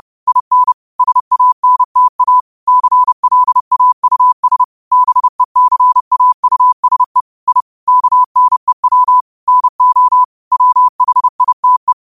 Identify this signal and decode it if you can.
Morse code, heard as a keyed tone, about 30 wpm, 1000 Hz. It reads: IN SANTA CLAUS BECAUSE I KNEW NO WHITE
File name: qotdfiles / 30wpm